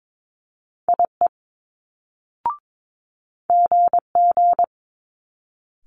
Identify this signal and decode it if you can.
Morse code, heard as a keyed tone, about 22 wpm, 700 Hz. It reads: IE GG
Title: Multi-letter Phonemes Quick Fire 22wpm